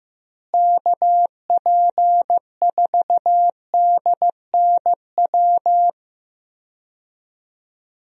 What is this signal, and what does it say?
Morse code, heard as a keyed tone, about 15 wpm, 700 Hz. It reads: KP4DNW
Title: USA Call Signs - 6 Characters - Repeated 3x - 15wpm